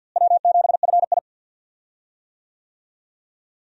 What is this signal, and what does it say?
Morse code, heard as a keyed tone, about 52 wpm, 700 Hz. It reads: W7FI
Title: USA Call Signs - 4 Characters - Repeated 3x - 50wpm